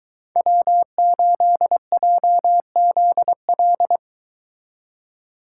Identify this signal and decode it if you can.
Morse code, heard as a keyed tone, about 22 wpm, 700 Hz. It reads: W8JZL